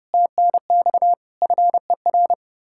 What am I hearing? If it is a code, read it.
Morse code, heard as a keyed tone, about 40 wpm, 700 Hz. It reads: TNX FER